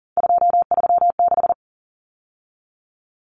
Morse code, heard as a keyed tone, about 40 wpm, 700 Hz. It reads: 236
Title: Numbers - 3-Digit 40wpm